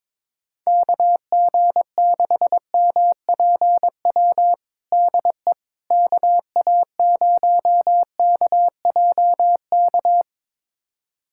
Morse code, heard as a keyed tone, about 21 wpm, 700 Hz. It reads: KG6MPW DE KA0KJK
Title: USA Callsign DE Callsign - 6 Characters - 22wpm